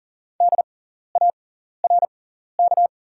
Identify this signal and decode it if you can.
Morse code, heard as a keyed tone, about 39 wpm, spaced as for 14 wpm, 690 Hz. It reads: DARK
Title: 6x Character Spacing - Top 500 Words - 40wpm